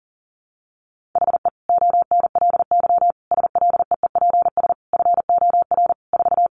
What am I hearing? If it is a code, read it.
Morse code, heard as a keyed tone, about 40 wpm, 700 Hz. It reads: HE ONLY SLEEPS FOR 4
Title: Additional Easy Phrases 40wpm